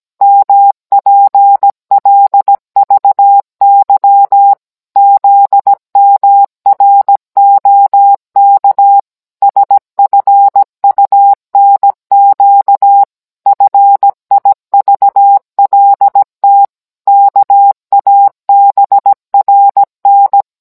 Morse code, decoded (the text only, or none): MPLVY ZMROK SFUNQ FIVLT KABRN